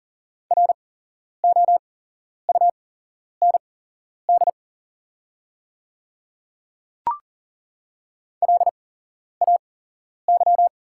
Morse code, heard as a keyed tone, about 39 wpm, spaced as for 11 wpm, 690 Hz. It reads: ROUND LAY
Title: 8x Character Spacing - Top 500 Words - 40wpm